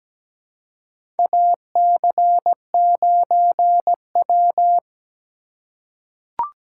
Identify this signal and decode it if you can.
Morse code, heard as a keyed tone, about 17 wpm, 700 Hz. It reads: AC9W
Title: USA Call Signs - 4 Characters - 3x Character Spacing - 17wpm